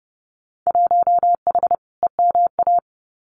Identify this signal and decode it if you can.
Morse code, heard as a keyed tone, about 30 wpm, 700 Hz. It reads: 1H EMA
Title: Winter Field Day Exchanges 30wpm